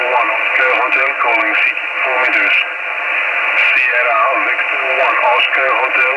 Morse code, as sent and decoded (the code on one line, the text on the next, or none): none